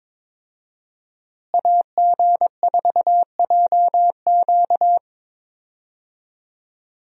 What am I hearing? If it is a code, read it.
Morse code, heard as a keyed tone, about 21 wpm, 700 Hz. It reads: AG4JQ
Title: USA Call Signs - 5 Characters - Repeated 2x - 22wpm